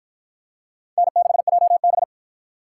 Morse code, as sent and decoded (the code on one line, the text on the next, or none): -. -.... .--- -...
N6JB